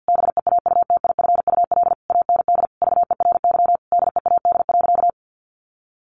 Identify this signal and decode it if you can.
Morse code, heard as a keyed tone, about 50 wpm, 700 Hz. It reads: BEAUTIFUL AND VERY DEAD.
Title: A Sound Of Thunder 50wpm